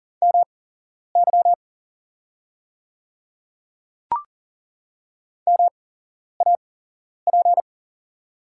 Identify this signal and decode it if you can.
Morse code, heard as a keyed tone, about 40 wpm, spaced as for 11 wpm, 700 Hz. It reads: MY MAP
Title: 8x Character Spacing - Top 500 Words - 40wpm